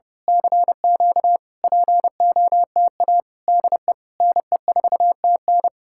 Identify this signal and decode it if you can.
Morse code, heard as a keyed tone, about 28 wpm, 700 Hz. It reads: CQ POTA DE NE4TN
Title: Speed-Racing Common QSO Phrases - 30wpm